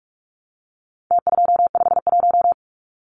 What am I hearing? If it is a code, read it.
Morse code, heard as a keyed tone, about 44 wpm, 700 Hz. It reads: T251
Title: Numbers - Contest Style - 4-Digit 45wpm